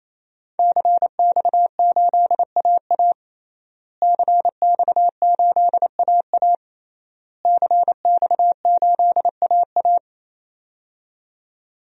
Morse code, heard as a keyed tone, about 28 wpm, 700 Hz. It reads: CX8AA CX8AA CX8AA
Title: DXCC-100 Call Signs - 5 Characters - Repeated 3x - 28wpm